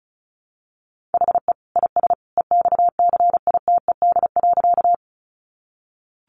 Morse code, heard as a keyed tone, about 34 wpm, 700 Hz. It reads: HE IS EXCITED.
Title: Additional Easy Phrases 35wpm